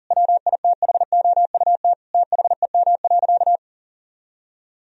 Morse code, heard as a keyed tone, about 40 wpm, 700 Hz. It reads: WITHOUT THEM.
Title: Sentences from Top 1000 Words - Review Part 3 - 40wpm